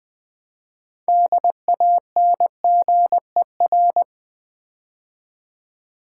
Morse code, heard as a keyed tone, about 20 wpm, 690 Hz. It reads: DANGER
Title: Build A Word - 2 Letters At A Time - 6-Letter Words - 20wpm